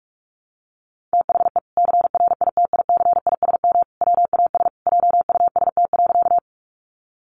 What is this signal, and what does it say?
Morse code, heard as a keyed tone, about 44 wpm, 700 Hz. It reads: THE CRITICISM WAS JUST.
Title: Sentences from Top 1300 Words 45wpm